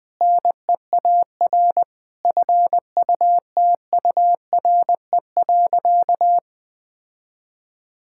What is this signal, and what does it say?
Morse code, heard as a keyed tone, about 20 wpm, 700 Hz. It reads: NEAR FUTURE.
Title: Sentences with 7-Word Phrases - Spoken Beginning - 20wpm